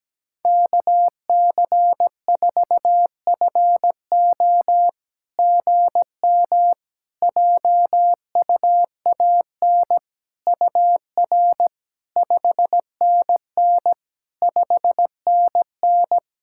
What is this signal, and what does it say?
Morse code, heard as a keyed tone, about 17 wpm, 690 Hz. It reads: KC4FO GM JUAN UR 5NN 5NN
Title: SOTA QSO Practice 17wpm